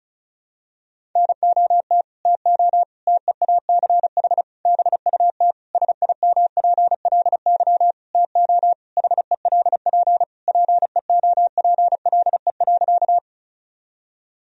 Morse code, heard as a keyed tone, about 36 wpm, 700 Hz. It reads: NOT TO TEACH BUT SIMPLY TO HELP PEOPLE.
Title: Sentences from Top 1000 Words - Review Part 4 - 35wpm